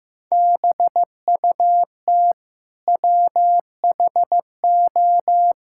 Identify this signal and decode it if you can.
Morse code, heard as a keyed tone, about 16 wpm, 700 Hz. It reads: BUT WHO
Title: Sentences from Top 1000 Words - Review Part 5 - 15wpm